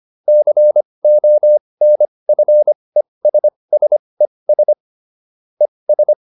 Morse code, heard as a keyed tone, about 25 wpm, spaced as for 20 wpm, 600 Hz. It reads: CONFESSES ES
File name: ES8 20